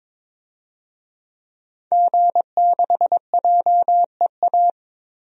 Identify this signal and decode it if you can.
Morse code, heard as a keyed tone, about 22 wpm, 700 Hz. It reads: G6JEA
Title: DXCC-100 Call Signs - 5 Characters - Repeated 3x - 22wpm